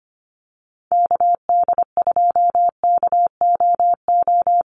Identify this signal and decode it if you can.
Morse code, heard as a keyed tone, about 25 wpm, 700 Hz. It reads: KD2KOO
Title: USA Call Signs 25wpm